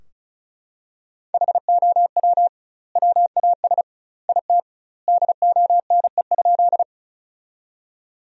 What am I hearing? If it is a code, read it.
Morse code, heard as a keyed tone, about 36 wpm, 700 Hz. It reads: HOW WAS IT DONE?
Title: 2x Word-Spacing - Sentences from Top 500 Words - 35wpm